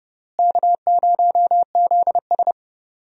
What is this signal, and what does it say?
Morse code, heard as a keyed tone, about 30 wpm, 700 Hz. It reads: K0ZS
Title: USA Call Signs - 4 Characters - Repeated 2x - 30wpm